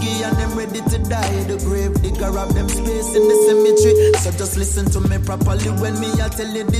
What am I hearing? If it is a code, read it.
Ringback tone (Europe, 425 Hz), with noise.